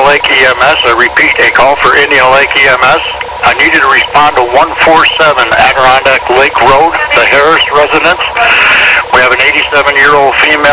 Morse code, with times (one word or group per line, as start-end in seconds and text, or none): none